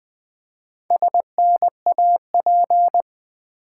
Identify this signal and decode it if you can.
Morse code, heard as a keyed tone, about 20 wpm, 700 Hz. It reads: SNAP
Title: Build A Word - 2 Letters At A Time - 4-Letter Words - 20wpm